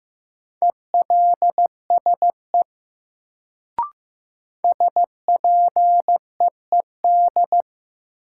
Morse code, heard as a keyed tone, about 15 wpm, 700 Hz. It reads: ELSE SPEED
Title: Sentences from Top 800 Words - New Word Drill - 15wpm